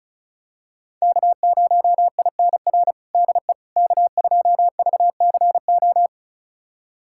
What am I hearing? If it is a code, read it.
Morse code, heard as a keyed tone, about 36 wpm, 700 Hz. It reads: K0INR DE K2VCO
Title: USA Callsign DE Callsign - 5 Characters - 35wpm